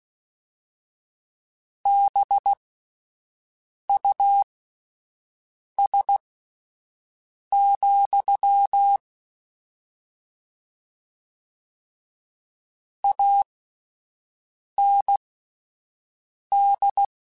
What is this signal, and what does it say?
Morse code, heard as a keyed tone, about 16 wpm, spaced as for 5 wpm, 800 Hz. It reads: BUS, AND